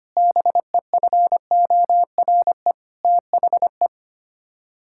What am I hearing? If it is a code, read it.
Morse code, heard as a keyed tone, about 25 wpm, 700 Hz. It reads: BEFORE THE
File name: Sets of 9 Words - 25wpm